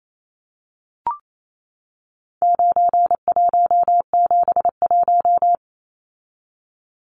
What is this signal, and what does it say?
Morse code, heard as a keyed tone, about 28 wpm, 700 Hz. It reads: 9171
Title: Numbers - 4-Digit 28wpm